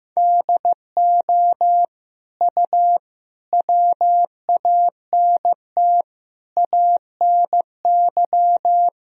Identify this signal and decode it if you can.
Morse code, heard as a keyed tone, about 15 wpm, 700 Hz. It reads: DO U WANT ANY
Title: Sentences from Top 400 Words and Abbreviations 15wpm